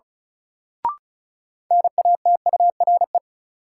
Morse code, heard as a keyed tone, about 34 wpm, 700 Hz. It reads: NATURE
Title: Sets of 3 Words - New Word Drill - 35wpm